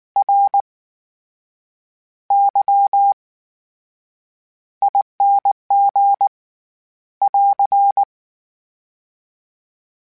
Morse code, heard as a keyed tone, about 19 wpm, spaced as for 8 wpm, 800 Hz. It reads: RYING <AR>
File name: Special INGs / Special INGs-mp3s / VerySlowToFaster Ing @18